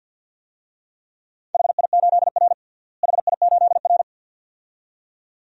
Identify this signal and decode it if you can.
Morse code, heard as a keyed tone, about 52 wpm, 700 Hz. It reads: HI8R HI8R
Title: DXCC-100 Call Signs - 4 Characters - Repeated 2x - 50wpm